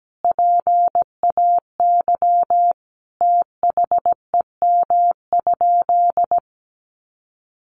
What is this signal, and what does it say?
Morse code, heard as a keyed tone, about 17 wpm, 700 Hz. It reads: PAY THEM?
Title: Sentences from Top 700 Words 17wpm